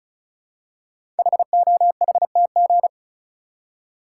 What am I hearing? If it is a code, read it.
Morse code, heard as a keyed tone, about 36 wpm, 700 Hz. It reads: HOHTG
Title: Code Groups - 5 Letters - 35wpm